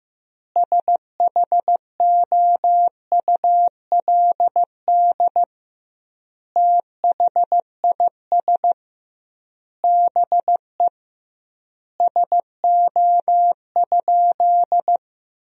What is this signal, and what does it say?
Morse code, heard as a keyed tone, about 16 wpm, 700 Hz. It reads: SHOULD THIS BE SO?
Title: 2x Word-Spacing - Sentences from Top 300 Words - 15wpm